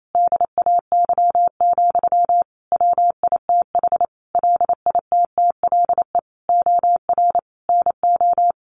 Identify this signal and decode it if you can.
Morse code, heard as a keyed tone, about 28 wpm, 700 Hz. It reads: DAY, WITH LITTLE OR NO
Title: Sentences from Top 1400 Words 28wpm